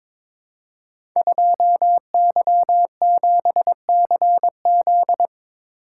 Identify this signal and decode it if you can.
Morse code, heard as a keyed tone, about 22 wpm, 690 Hz. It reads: 2Y7CZ
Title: Code Groups - 5 Letter:Numbers - 22wpm